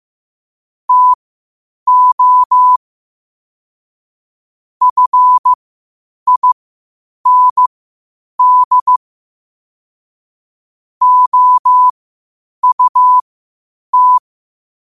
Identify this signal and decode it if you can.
Morse code, heard as a keyed tone, about 15 wpm, spaced as for 8 wpm, 1000 Hz. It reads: TO FIND OUT
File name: 8wpm